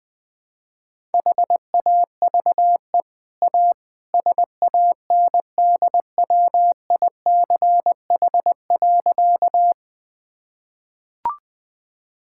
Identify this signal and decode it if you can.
Morse code, heard as a keyed tone, about 20 wpm, 700 Hz. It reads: HAVE A SANDWICH.
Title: Easy Sentences with Hints - 20wpm